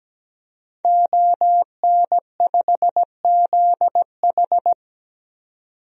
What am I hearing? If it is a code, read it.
Morse code, heard as a keyed tone, about 17 wpm, 700 Hz. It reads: ON5ZH
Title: DXCC-100 Call Signs - 5 Characters - 17wpm